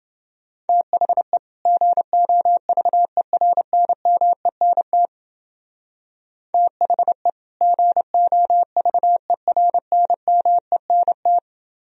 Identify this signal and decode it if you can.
Morse code, heard as a keyed tone, about 28 wpm, 700 Hz. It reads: THE GOVERNMENT THE GOVERNMENT